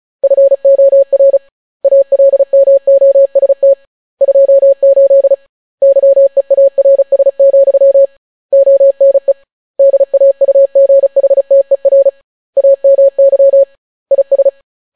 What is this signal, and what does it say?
Morse code, heard as a keyed tone, about 36 wpm, 550 Hz. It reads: FOR ALMOST 28 YEARS, ONE DAUGHTER AMY IS